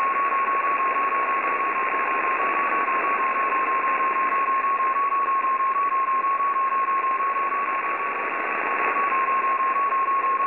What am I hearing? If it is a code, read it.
FT8.
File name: F3-BP-2020-Horacek-Jakub-priloha-LDPC WebSDR France